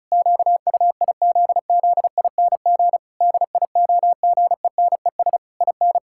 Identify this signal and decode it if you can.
Morse code, heard as a keyed tone, about 36 wpm, 700 Hz. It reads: QUIZZING DIOGENES IN
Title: Pangram Challenge v2 - 35wpm